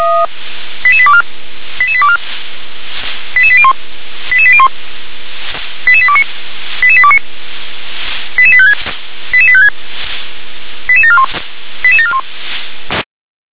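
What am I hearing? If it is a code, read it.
Five-tone selective call (ZVEI), heard as quick runs of five tones, tones of about 70 ms. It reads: 88824 88824 88801 88801 88829 88829 88856 88856 88741 88741
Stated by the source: ZVEI five-tone call 88835 received on an SDR.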